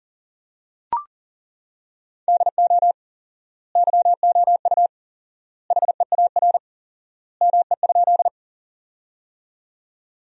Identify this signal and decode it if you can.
Morse code, heard as a keyed tone, about 40 wpm, 700 Hz. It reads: DO YOU HEAR ME?
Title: 4x Word-Spacing - Sentences from Top 300 Words - 40wpm